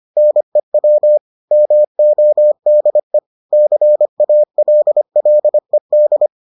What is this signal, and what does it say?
Morse code, heard as a keyed tone, about 25 wpm, 600 Hz. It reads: NEW MODE CALLED